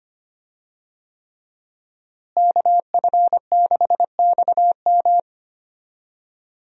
Morse code, heard as a keyed tone, about 25 wpm, 700 Hz. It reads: KF6XM
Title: USA Call Signs - 5 Characters - Repeated 3x - 25wpm